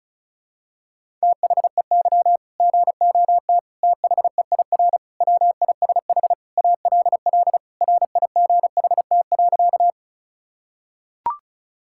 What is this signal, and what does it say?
Morse code, heard as a keyed tone, about 36 wpm, 700 Hz. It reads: THEY GOT THEIR WISH ALL RIGHT.
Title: Sentences from Top 1600 Words 35wpm